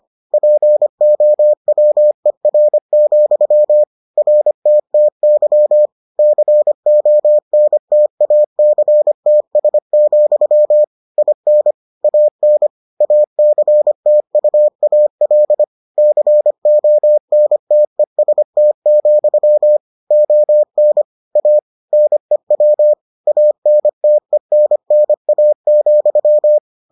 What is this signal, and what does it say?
Morse code, heard as a keyed tone, about 25 wpm, 600 Hz. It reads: POWER, RTTY CONTACTS, IN AN ACTUAL CONTEST, ON A NEW ANTENNA,